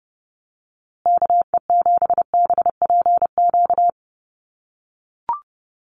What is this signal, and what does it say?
Morse code, heard as a keyed tone, about 29 wpm, 700 Hz. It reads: KE7BPQ
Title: USA Call Signs Repeated 2x 30wpm